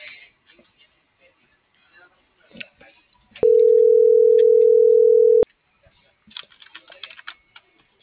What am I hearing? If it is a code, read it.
Ringback tone (North America), 440 + 480 Hz.